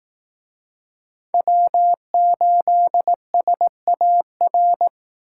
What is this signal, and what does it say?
Morse code, heard as a keyed tone, about 18 wpm, 700 Hz. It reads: W8SAR
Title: Speed Racing - USA 1x3 Call Signs 15wpm